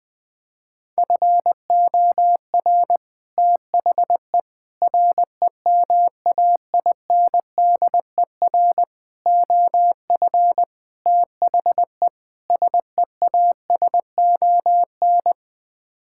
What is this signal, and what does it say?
Morse code, heard as a keyed tone, about 20 wpm, 700 Hz. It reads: FOR THE REMAINDER OF THE SEASON
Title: Sets of 6 Words - 20wpm